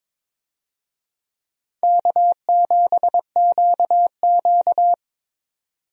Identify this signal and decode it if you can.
Morse code, heard as a keyed tone, about 22 wpm, 700 Hz. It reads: K7QQ
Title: USA Call Signs - 4 Characters - 22wpm